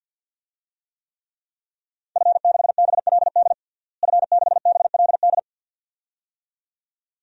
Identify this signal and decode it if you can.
Morse code, heard as a keyed tone, about 51 wpm, 700 Hz. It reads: F6BLD F6BLD
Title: DXCC-100 Call Signs - 5 Characters - Repeated 2x - 50wpm